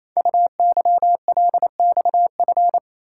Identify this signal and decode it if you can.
Morse code, heard as a keyed tone, about 29 wpm, 700 Hz. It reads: UYLXF
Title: Code Groups - 5 Letters - 28wpm